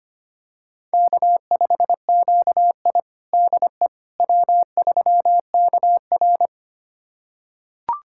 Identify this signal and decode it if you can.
Morse code, heard as a keyed tone, about 25 wpm, 700 Hz. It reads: K5QI DE W3KR
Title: USA Callsign DE Callsign - 4 Characters - 25wpm